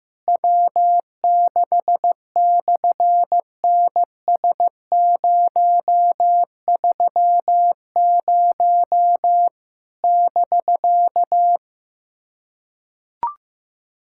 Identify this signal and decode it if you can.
Morse code, heard as a keyed tone, about 14 wpm, 700 Hz. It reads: W6/NS030 <BK>
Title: SOTA QSO Practice 15wpm